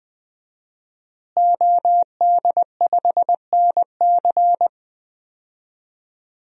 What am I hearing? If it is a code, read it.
Morse code, heard as a keyed tone, about 20 wpm, 700 Hz. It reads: OD5NC
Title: DXCC-100 Call Signs - 5 Characters - Repeated 2x - 20wpm